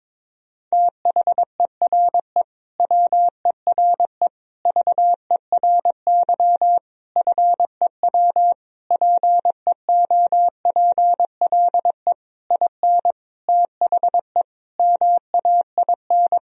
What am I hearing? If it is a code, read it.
Morse code, heard as a keyed tone, about 22 wpm, 700 Hz. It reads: THERE WERE VERY FEW PEOPLE IN THE MAIN
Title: Sentences from Top 1000 Words - Review Part 2 - 22wpm